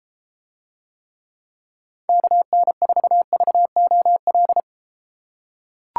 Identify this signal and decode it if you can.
Morse code, heard as a keyed tone, about 32 wpm, 700 Hz. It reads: KN4VOL